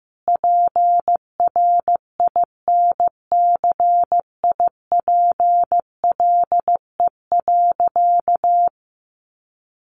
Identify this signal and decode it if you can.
Morse code, heard as a keyed tone, about 15 wpm, 700 Hz. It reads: PRINCIPLE.
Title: Sentences from Top 1400 Words 15wpm